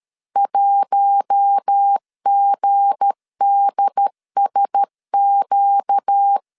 Morse code, heard as a keyed tone, about 13 wpm, 800 Hz. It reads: 1GDSQ
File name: LEZIONI / 7 GF9Z / 7 GF9Z 12wpm 2147